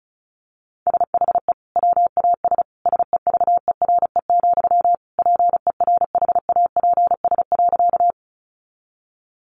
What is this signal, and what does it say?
Morse code, heard as a keyed tone, about 34 wpm, 700 Hz. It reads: SHE WAS SEVERE, PERHAPS.
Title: Sentences from Top 1400 Words 35wpm